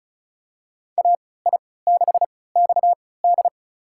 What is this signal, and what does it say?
Morse code, heard as a keyed tone, about 36 wpm, spaced as for 20 wpm, 700 Hz. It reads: AI6XD
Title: USA Call Signs - 5 Characters - 3x Character Spacing - 35wpm